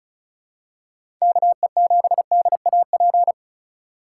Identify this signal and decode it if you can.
Morse code, heard as a keyed tone, about 34 wpm, 700 Hz. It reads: KE7DAP
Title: USA Call Signs - 4x Character Spacing - 35wpm